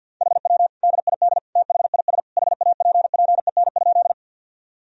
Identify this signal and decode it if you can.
Morse code, heard as a keyed tone, about 50 wpm, 700 Hz. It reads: HW DID THIS HAPPEN?
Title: Sentences from Top 400 Words and Abbreviations 50wpm